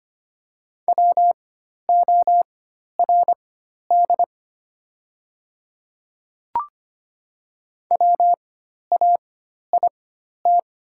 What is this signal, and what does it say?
Morse code, heard as a keyed tone, about 25 wpm, spaced as for 12 wpm, 700 Hz. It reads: WORD WAIT
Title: 4x Character Spacing - Top 500 Words - 25wpm